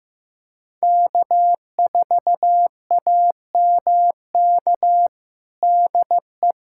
Morse code, heard as a keyed tone, about 15 wpm, 700 Hz. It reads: K4AMK DE
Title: USA Callsign DE Callsign - 5 Characters - 15wpm